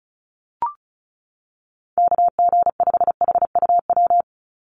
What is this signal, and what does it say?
Morse code, heard as a keyed tone, about 34 wpm, 700 Hz. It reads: KG5HUW
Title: USA Call Signs 35wpm